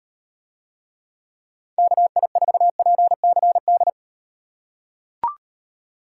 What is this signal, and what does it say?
Morse code, heard as a keyed tone, about 38 wpm, 700 Hz. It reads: KI4PCD